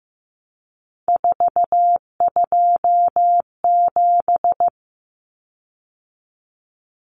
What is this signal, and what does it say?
Morse code, heard as a keyed tone, about 15 wpm, 700 Hz. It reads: 427
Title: Common Cut Numbers - 3-Digit 15wpm